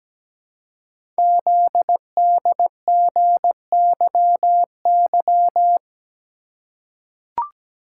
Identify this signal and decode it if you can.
Morse code, heard as a keyed tone, about 17 wpm, 700 Hz. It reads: ZDGYY
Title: Code Groups - 5 Letters - 17wpm